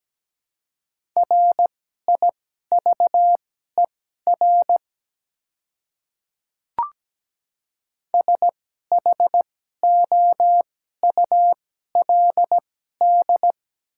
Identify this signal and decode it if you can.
Morse code, heard as a keyed tone, about 17 wpm, spaced as for 12 wpm, 700 Hz. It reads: RIVER SHOULD
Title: 2x Character Spacing - Top 300 Words - 17wpm